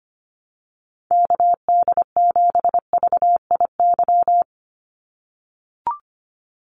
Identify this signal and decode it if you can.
Morse code, heard as a keyed tone, about 25 wpm, 700 Hz. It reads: KD7VIY
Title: USA Call Signs 25wpm